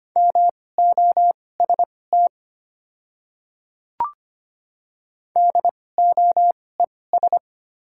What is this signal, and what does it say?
Morse code, heard as a keyed tone, about 25 wpm, spaced as for 18 wpm, 690 Hz. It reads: MOST DOES
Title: 2x Character Spacing - Top 200 Words - 25wpm